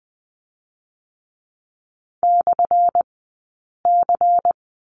Morse code, heard as a keyed tone, about 20 wpm, spaced as for 8 wpm, 700 Hz. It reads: /C
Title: 177 Lesson 29d - Slash - Two Characters 20wpm